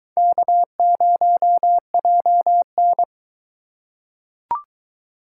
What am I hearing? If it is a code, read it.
Morse code, heard as a keyed tone, about 21 wpm, 700 Hz. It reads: K0JN